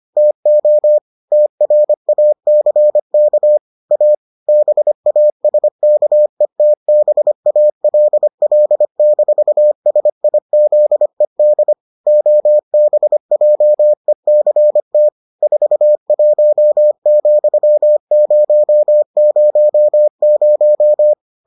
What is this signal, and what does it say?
Morse code, heard as a keyed tone, about 25 wpm, 600 Hz. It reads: TO TRACK A BASKETBALL-SIZED OBJECT 41,000